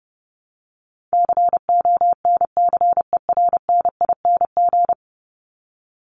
Morse code, heard as a keyed tone, about 30 wpm, 700 Hz. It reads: CONCERNING
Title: Top 1751-2000 Words 30wpm